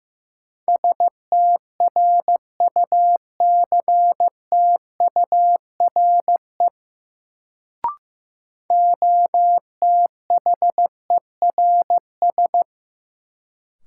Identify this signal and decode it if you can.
Morse code, heard as a keyed tone, about 15 wpm, 700 Hz. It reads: STRUCTURE OTHERS
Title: Sentences from Top 1500 Words - New Word Drill Alt - 15wpm